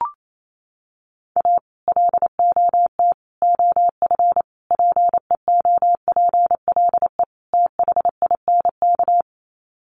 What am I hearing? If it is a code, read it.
Morse code, heard as a keyed tone, about 28 wpm, 700 Hz. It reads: A LOT OF PEOPLE THINK
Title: Sets of 5 Words Encore 28wpm